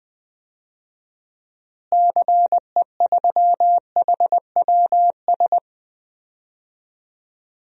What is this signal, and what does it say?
Morse code, heard as a keyed tone, about 20 wpm, 700 Hz. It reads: CE3HWS
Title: DXCC-100 Call Signs - 6 Characters - Repeated 3x - 20wpm